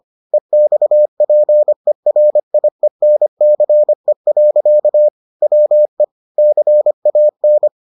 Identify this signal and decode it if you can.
Morse code, heard as a keyed tone, about 25 wpm, 600 Hz. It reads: EXPERIENCE. WE CAN